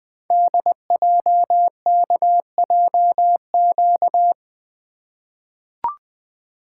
Morse code, heard as a keyed tone, about 20 wpm, 700 Hz. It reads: DJKJQ
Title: Code Groups - 5 Letters - 20wpm